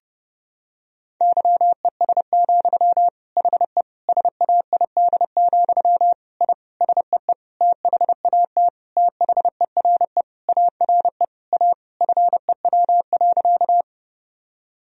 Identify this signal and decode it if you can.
Morse code, heard as a keyed tone, about 32 wpm, 700 Hz. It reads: YES, HE SAID, I SEE THAT THERE ARE A FEW.
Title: Sentences from Top 1000 Words - Review Part 1 - 30wpm